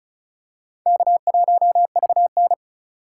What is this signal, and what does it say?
Morse code, heard as a keyed tone, about 36 wpm, 700 Hz. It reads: K1VN
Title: USA Call Signs - 4 Characters - 4x Character Spacing - 35wpm